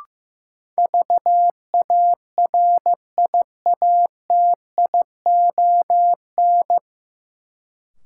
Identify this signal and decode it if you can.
Morse code, heard as a keyed tone, about 15 wpm, 700 Hz. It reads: VARIATION
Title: Sentences from Top 1600 Words - New Word Drill Alt - 15wpm